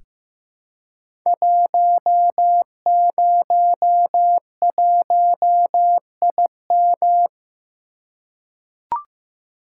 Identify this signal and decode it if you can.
Morse code, heard as a keyed tone, about 15 wpm, 700 Hz. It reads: 101IM
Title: Code Groups - 5 Letter:Numbers - 15wpm